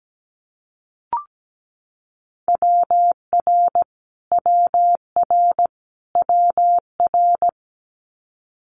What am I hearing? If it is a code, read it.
Morse code, heard as a keyed tone, about 17 wpm, 700 Hz. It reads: WR WR WR
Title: Multi-letter Phonemes Repeated 3x 17wpm